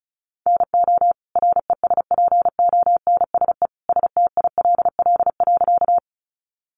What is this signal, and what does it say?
Morse code, heard as a keyed tone, about 34 wpm, 700 Hz. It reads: NO RESPONSE STILL.
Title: Sentences from Top 1400 Words 35wpm